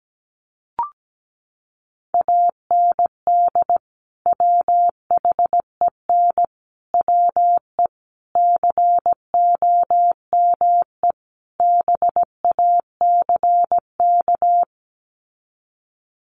Morse code, heard as a keyed tone, about 17 wpm, 700 Hz. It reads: AND WHEN WE COME BACK
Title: Sets of 5 Words 17wpm